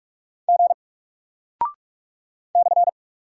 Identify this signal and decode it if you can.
Morse code, heard as a keyed tone, about 44 wpm, 700 Hz. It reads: G /
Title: Single Character (Letter:Number:Punctuation:Prosign) 45wpm